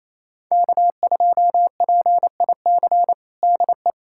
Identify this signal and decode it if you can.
Morse code, heard as a keyed tone, about 28 wpm, 700 Hz. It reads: K2PIC DE
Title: USA Callsign DE Callsign 28wpm